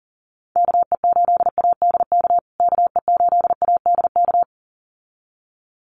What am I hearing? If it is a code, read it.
Morse code, heard as a keyed tone, about 40 wpm, 700 Hz. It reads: KE8ADK KE8ADK
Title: USA Call Signs Repeated 2x 40wpm